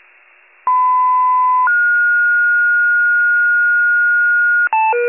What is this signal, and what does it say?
Two-tone page, heard as a one-second tone, then a three-second tone, about 1010 then 1430 Hz. It is heard from 0.7 to 4.7 s.
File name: wcalert